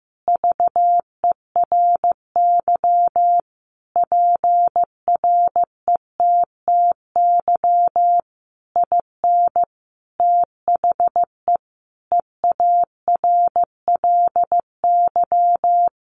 Morse code, heard as a keyed tone, about 15 wpm, 700 Hz. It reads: VERY PRETTY IN THE EARLY
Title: Sentences from Top 1300 Words 15wpm